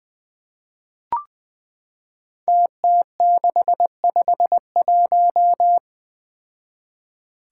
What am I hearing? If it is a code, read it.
Morse code, heard as a keyed tone, about 20 wpm, 700 Hz. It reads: TT651
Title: Five Challenging Letter-Number Combinations 20wpm